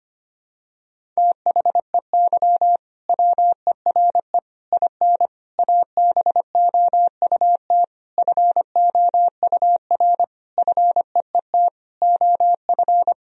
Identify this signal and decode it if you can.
Morse code, heard as a keyed tone, about 25 wpm, 700 Hz. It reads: THEY WERE IN ABOUT FOUR FEET OF
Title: Sentences from Top 1000 Words - Review Part 1 - 25wpm